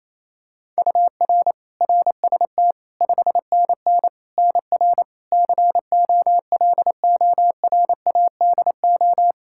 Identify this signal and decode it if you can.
Morse code, heard as a keyed tone, about 28 wpm, 700 Hz. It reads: UR RST 5NN NR COLORADO
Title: Practice QSO - Exchange 1 - Beginning - 28wpm